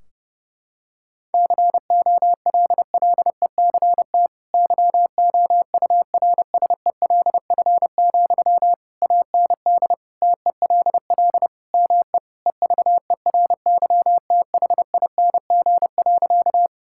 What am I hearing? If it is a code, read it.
Morse code, heard as a keyed tone, about 30 wpm, 700 Hz. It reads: COLLECT YOURSELF, AND TELL ME EVERYTHING.
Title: Sentences from Top 1600 Words 30wpm